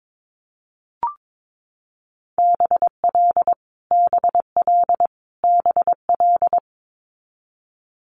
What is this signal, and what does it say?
Morse code, heard as a keyed tone, about 22 wpm, 700 Hz. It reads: BL BL BL
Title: Multi-letter Phonemes Repeated 3x 22wpm